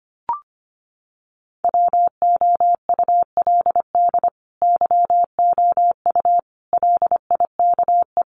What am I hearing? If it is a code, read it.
Morse code, heard as a keyed tone, about 25 wpm, 700 Hz. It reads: WOULD YOU LIKE
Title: Sentences from Top 100 Words 25wpm